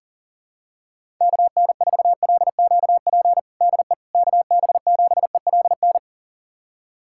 Morse code, heard as a keyed tone, about 40 wpm, 700 Hz. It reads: KN4LQP DE KB7ELN
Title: USA Callsign DE Callsign 40wpm